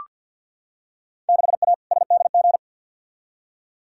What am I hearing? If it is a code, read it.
Morse code, heard as a keyed tone, about 46 wpm, 700 Hz. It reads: 6A SDG
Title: Field Day Exchanges Encore - 50wpm